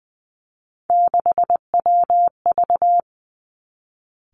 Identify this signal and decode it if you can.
Morse code, heard as a keyed tone, about 20 wpm, 700 Hz. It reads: 6WV
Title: Three Challenging Letter-Number Combinations 20wpm